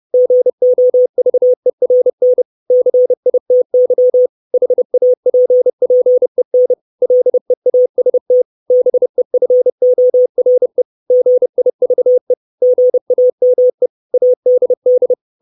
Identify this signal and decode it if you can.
Morse code, heard as a keyed tone, about 30 wpm, 500 Hz. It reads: GOVERN CITY HAPPEN LEAST BEFORE GIVE GAME ADD
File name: 632 Words 30WPM